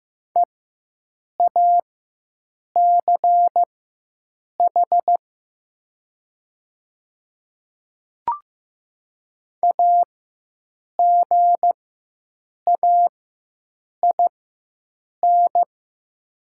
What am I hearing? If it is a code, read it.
Morse code, heard as a keyed tone, about 15 wpm, spaced as for 7 wpm, 700 Hz. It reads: EACH AGAIN
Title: 4x Character Spacing - Top 200 Words - 15wpm